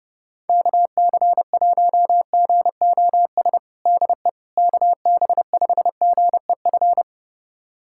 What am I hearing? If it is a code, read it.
Morse code, heard as a keyed tone, about 30 wpm, 700 Hz. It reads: KC1GOS DE KB5GEF
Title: USA Callsign DE Callsign - 6 Characters - 30wpm